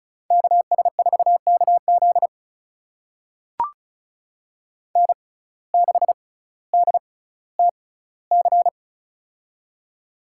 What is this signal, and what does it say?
Morse code, heard as a keyed tone, about 34 wpm, spaced as for 18 wpm, 700 Hz. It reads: KS4KZ N6DTC
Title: USA Call Signs - 6x Character Spacing - 35wpm